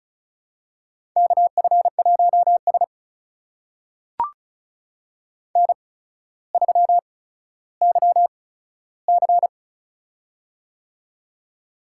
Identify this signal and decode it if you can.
Morse code, heard as a keyed tone, about 36 wpm, spaced as for 15 wpm, 700 Hz. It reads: KF1S N3YC
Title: USA Call Signs - 4 Characters - 8x Character Spacing - 35wpm